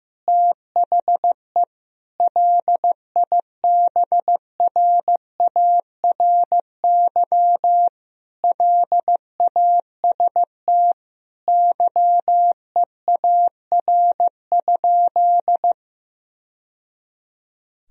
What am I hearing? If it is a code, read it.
Morse code, heard as a keyed tone, about 15 wpm, 700 Hz. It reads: THE LIBRARY LAST YEAR?
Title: Sentences from Top 1000 Words - Review Part 4 - 15wpm